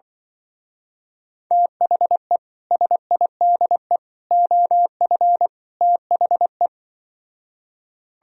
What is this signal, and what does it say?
Morse code, heard as a keyed tone, about 24 wpm, 700 Hz. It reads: THE SIDE OF THE
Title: Speed-Racing - Sets of 4 Words 20wpm